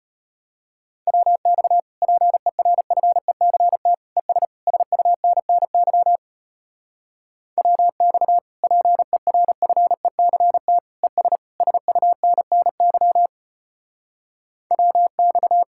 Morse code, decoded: WX PERFECT ES SUNNY WX PERFECT ES SUNNY WX

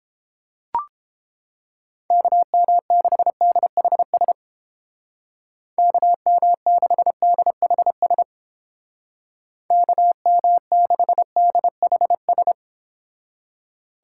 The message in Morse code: KM6DHS KM6DHS KM6DHS

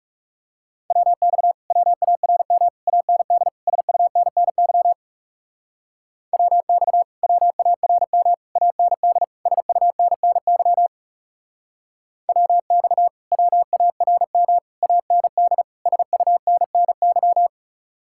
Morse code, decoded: WX WARM AND SUNNY WX WARM AND SUNNY WX WARM AND SUNNY

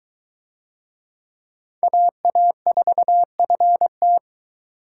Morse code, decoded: AA4FT

